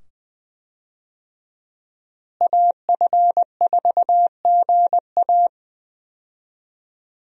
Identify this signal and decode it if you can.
Morse code, heard as a keyed tone, about 20 wpm, 700 Hz. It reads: AF4GA
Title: USA Call Signs - 5 Characters - Repeated 2x - 20wpm